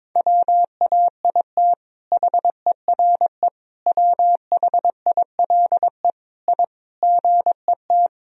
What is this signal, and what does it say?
Morse code, heard as a keyed tone, about 22 wpm, 700 Hz. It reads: WAIT HERE WHILE I GET
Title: Sentences from Top 1000 Words - Review Part 4 - 22wpm